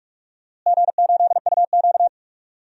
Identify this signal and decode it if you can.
Morse code, heard as a keyed tone, about 42 wpm, 700 Hz. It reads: G8UQ
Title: DXCC-100 Call Signs - 4 Characters - Repeated 2x - 45wpm